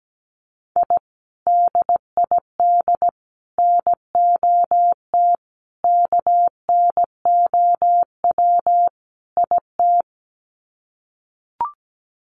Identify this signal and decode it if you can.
Morse code, heard as a keyed tone, about 17 wpm, 700 Hz. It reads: I DID NOT KNOW IT
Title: Sets of 5 Words Encore2 17wpm